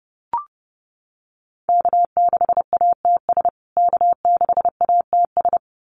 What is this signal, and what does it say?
Morse code, heard as a keyed tone, about 30 wpm, 700 Hz. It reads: K6ATS K6ATS
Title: USA Call Signs Repeated 2x 30wpm